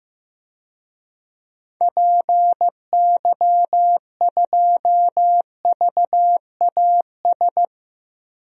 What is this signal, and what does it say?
Morse code, heard as a keyed tone, about 15 wpm, 700 Hz. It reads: PY2VAS
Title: DXCC-100 Call Signs - 6 Characters - Repeated 2x - 15wpm